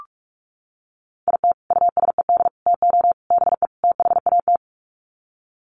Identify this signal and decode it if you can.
Morse code, heard as a keyed tone, about 44 wpm, 700 Hz. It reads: IT USED TO BE THAT
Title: Sets of 5 Words Encore2 45wpm